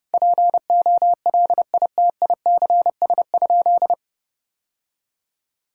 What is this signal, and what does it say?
Morse code, heard as a keyed tone, about 30 wpm, 700 Hz. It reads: POLITICS?
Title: Sentences from Top 1000 Words - Review Part 6 - 30wpm